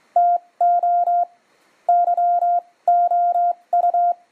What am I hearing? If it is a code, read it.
Morse code, heard as a keyed tone, about 25 wpm, spaced as for 19 wpm, 700 Hz. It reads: TO YOU
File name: f6uig hacksaw blade demo